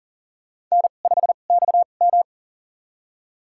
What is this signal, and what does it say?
Morse code, heard as a keyed tone, about 38 wpm, spaced as for 28 wpm, 700 Hz. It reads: N5XM
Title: USA Call Signs - 2x Character Spacing - 40wpm